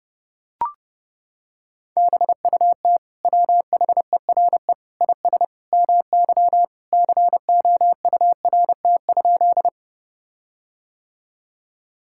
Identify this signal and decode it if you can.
Morse code, heard as a keyed tone, about 30 wpm, 700 Hz. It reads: BUT WHERE IS MY COURT?
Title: Sentences from Top 1600 Words 30wpm